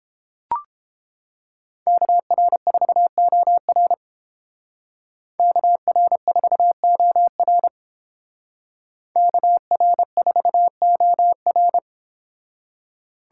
Morse code, heard as a keyed tone, about 30 wpm, 700 Hz. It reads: KR4OR KR4OR KR4OR